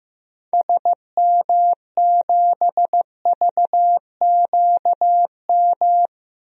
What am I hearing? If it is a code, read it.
Morse code, heard as a keyed tone, about 15 wpm, 700 Hz. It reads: SM7VQM
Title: DXCC-100 Call Signs - 6 Characters - Repeated 2x - 15wpm